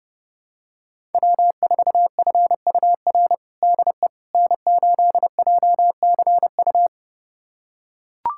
Morse code, heard as a keyed tone, about 30 wpm, 700 Hz. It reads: W4FUR DE N8JCU
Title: USA Callsign DE Callsign - 5 Characters - 30wpm